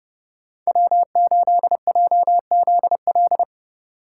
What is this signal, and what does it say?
Morse code, heard as a keyed tone, about 37 wpm, 700 Hz. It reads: W8JZL
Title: Speed Racing - USA 1x3 Call Signs 30wpm